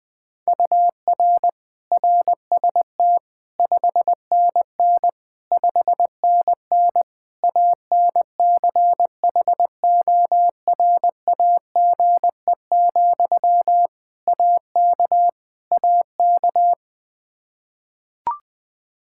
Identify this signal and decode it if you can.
Morse code, heard as a keyed tone, about 20 wpm, 700 Hz. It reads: UR RST 5NN 5NN ANCHORAGE, AK AK
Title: Practice QSO - Exchange 1 - Beginning - 20wpm